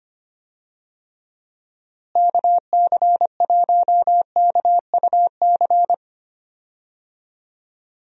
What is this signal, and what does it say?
Morse code, heard as a keyed tone, about 25 wpm, 700 Hz. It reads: KC1KUC
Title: USA Call Signs - 6 Characters - Repeated 3x - 25wpm